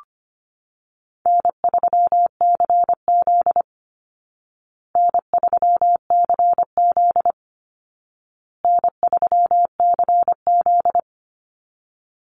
Morse code, heard as a keyed tone, about 25 wpm, 700 Hz. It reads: N3CZ N3CZ N3CZ
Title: Easy Call Signs Repeated 3x - 25wpm